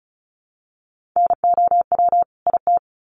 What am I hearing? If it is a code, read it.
Morse code, heard as a keyed tone, about 34 wpm, 700 Hz. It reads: NOW IT
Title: Sentences from Top 1300 Words 35wpm